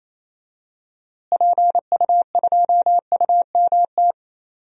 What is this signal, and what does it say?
Morse code, heard as a keyed tone, about 28 wpm, 700 Hz. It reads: PU2UMT
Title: DXCC-100 Call Signs - 6 Characters - Repeated 3x - 28wpm